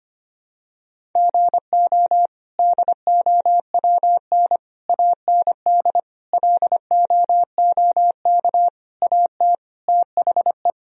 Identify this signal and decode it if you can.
Morse code, heard as a keyed tone, about 25 wpm, 700 Hz. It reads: GO DOWN AND LOOK AT THE
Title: Sentences from Top 1000 Words - Review Part 1 - 25wpm